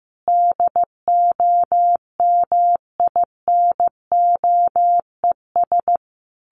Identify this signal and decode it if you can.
Morse code, heard as a keyed tone, about 15 wpm, 700 Hz. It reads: DOMINOES
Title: A Sound Of Thunder 15wpm